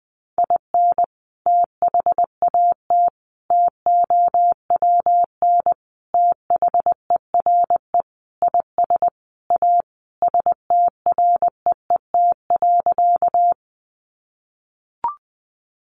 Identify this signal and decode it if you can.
Morse code, heard as a keyed tone, about 19 wpm, 700 Hz. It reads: IN THAT TOWN THERE IS A STREET.
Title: Sentences from Top 500 Words 20wpm